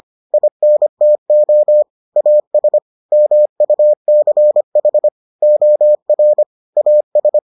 Morse code, heard as a keyed tone, about 25 wpm, 600 Hz. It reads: INTO AS MUCH OR AS